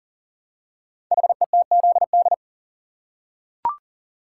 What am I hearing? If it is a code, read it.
Morse code, heard as a keyed tone, about 38 wpm, 700 Hz. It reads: HETZD
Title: Code Groups - 5 Letters - 40wpm